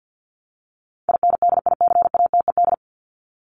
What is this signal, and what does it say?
Morse code, heard as a keyed tone, about 50 wpm, 700 Hz. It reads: INDICATED
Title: ED Suffix - Top 100 Words with Base Words 50wpm